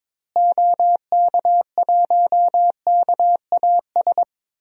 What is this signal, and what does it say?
Morse code, heard as a keyed tone, about 22 wpm, 700 Hz. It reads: OK1KAS
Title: DXCC-100 Call Signs - 6 Characters - Repeated 3x - 22wpm